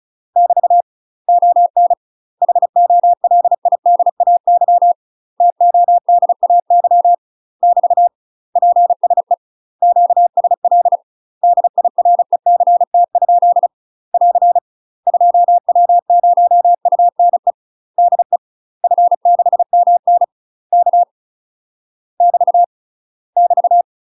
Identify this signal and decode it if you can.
Morse code, heard as a keyed tone, about 34 wpm, 700 Hz. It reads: <BT> ON HOLIDAY TODAY <BT> PSE QSL DIRECT? <AR> 2W0UNE DE F6MN K <BT> <BT>
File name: UK MASTER QSOs TRAINING / 35WPM / UK MASTER QSOs 0008